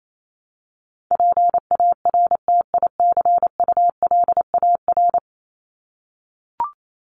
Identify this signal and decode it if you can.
Morse code, heard as a keyed tone, about 28 wpm, 700 Hz. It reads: PARTICULAR
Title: LY Suffix - Top 100 Words with Base Words 28wpm